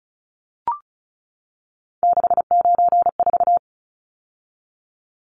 Morse code, heard as a keyed tone, about 34 wpm, 700 Hz. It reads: 694
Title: Numbers - 3-Digit 35wpm